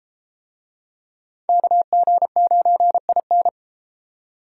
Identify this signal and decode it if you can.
Morse code, heard as a keyed tone, about 31 wpm, 700 Hz. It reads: KG9IN